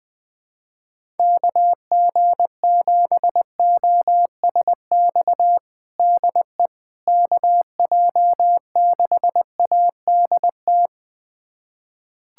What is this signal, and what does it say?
Morse code, heard as a keyed tone, about 21 wpm, 700 Hz. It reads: KG7OSX DE KJ6ADT
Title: USA Callsign DE Callsign 20wpm